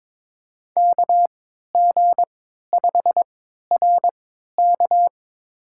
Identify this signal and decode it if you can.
Morse code, heard as a keyed tone, about 22 wpm, spaced as for 13 wpm, 700 Hz. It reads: KG5RK
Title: USA Call Signs - 5 Characters - 3x Character Spacing - 22wpm